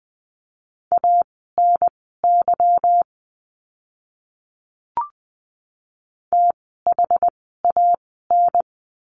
Morse code, heard as a keyed tone, about 20 wpm, spaced as for 15 wpm, 700 Hz. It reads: ANY THAN
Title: 146 Lesson 24f - Letter y - Words 20wpm